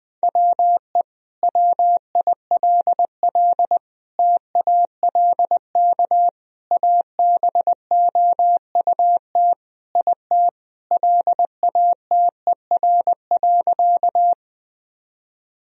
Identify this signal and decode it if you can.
Morse code, heard as a keyed tone, about 20 wpm, 700 Hz. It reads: WE WILL TALK ABOUT IT LATER.
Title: Sentences from Top 1000 Words - Review Part 4 - 20wpm